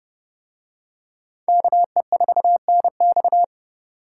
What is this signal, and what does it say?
Morse code, heard as a keyed tone, about 27 wpm, 700 Hz. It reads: KE4NX